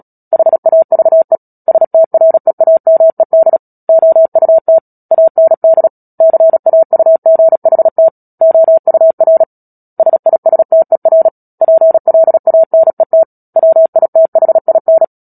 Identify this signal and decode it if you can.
Morse code, heard as a keyed tone, about 37 wpm, 650 Hz. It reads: HAVE STREAMED OUT AND CAUGHT OUR SISTER PLANET WITHIN
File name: Book Two / 10-Chapter 10 - Epilogue - 36 WPM